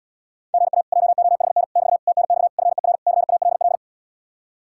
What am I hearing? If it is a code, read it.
Morse code, heard as a keyed tone, about 80 wpm, 700 Hz. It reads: GA JOHN <BT> TNX FER CALL